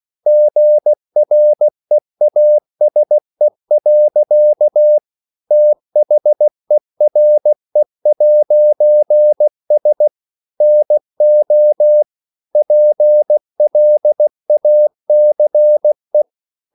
Morse code, heard as a keyed tone, about 16 wpm, 600 Hz. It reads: GREASE. THERE'S NO PLACE
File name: CW/Practice